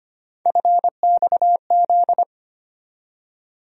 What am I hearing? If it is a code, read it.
Morse code, heard as a keyed tone, about 25 wpm, 700 Hz. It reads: FXZ
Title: USA Call Sign Postfixes - 3 Letters 25wpm